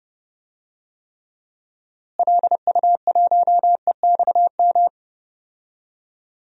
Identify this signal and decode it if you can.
Morse code, heard as a keyed tone, about 30 wpm, 700 Hz. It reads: LU1EXM
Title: DXCC-100 Call Signs - 6 Characters - Repeated 3x - 30wpm